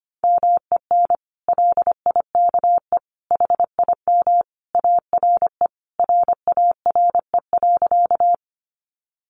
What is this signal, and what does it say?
Morse code, heard as a keyed tone, about 25 wpm, 700 Hz. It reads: MEN LIKE HIM ARE RARE.
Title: Sentences from Top 1100 Words 25wpm